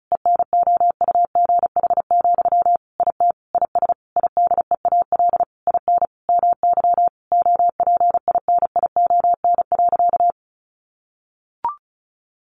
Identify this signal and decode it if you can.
Morse code, heard as a keyed tone, about 34 wpm, 700 Hz. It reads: ENOUGH, IT IS IDEAL IN MY OPINION.
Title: Sentences from Top 1400 Words 35wpm